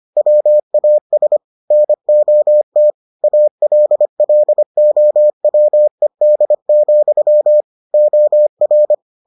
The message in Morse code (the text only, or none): WAS NOT ALLOWED, OR